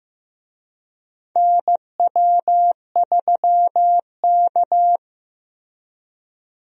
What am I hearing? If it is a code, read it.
Morse code, heard as a keyed tone, about 15 wpm, 700 Hz. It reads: NW3K
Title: USA Call Signs - 4 Characters - Repeated 2x - 15wpm